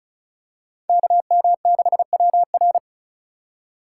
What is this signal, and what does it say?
Morse code, heard as a keyed tone, about 36 wpm, 700 Hz. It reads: KM6WR
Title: USA Call Signs - 5 Characters - 3x Character Spacing - 35wpm